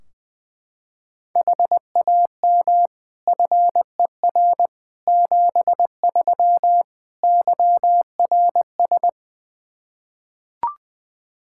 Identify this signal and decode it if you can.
Morse code, heard as a keyed tone, about 21 wpm, 700 Hz. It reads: HAM FER 73 YRS
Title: Practice QSO - Exchange 3 - More Info - 20wpm